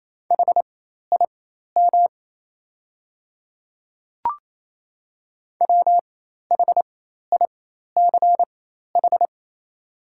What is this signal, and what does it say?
Morse code, heard as a keyed tone, about 28 wpm, spaced as for 13 wpm, 700 Hz. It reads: HIM WHICH
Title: 4x Character Spacing - Top 100 Words - 28wpm